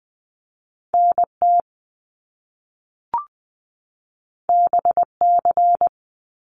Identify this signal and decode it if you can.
Morse code, heard as a keyed tone, about 20 wpm, 700 Hz. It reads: NT BC
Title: Canadian Provinces and Territories 20wpm